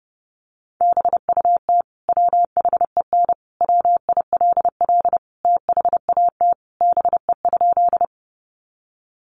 Morse code, heard as a keyed tone, about 30 wpm, 700 Hz. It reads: BUT WHEN WILL THAT BE?
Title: Sentences from Top 100 Words 30wpm